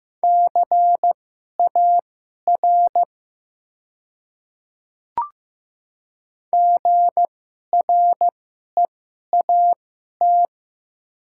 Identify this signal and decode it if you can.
Morse code, heard as a keyed tone, about 15 wpm, spaced as for 11 wpm, 700 Hz. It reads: CAR GREAT
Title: 2x Character Spacing - Top 300 Words - 15wpm